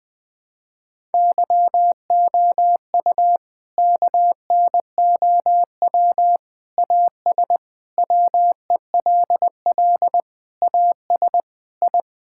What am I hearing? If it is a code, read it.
Morse code, heard as a keyed tone, about 21 wpm, 700 Hz. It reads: YOU KNOW AS WELL AS I
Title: Sentences with 8-Word Phrases - Spoken Beginning - 20wpm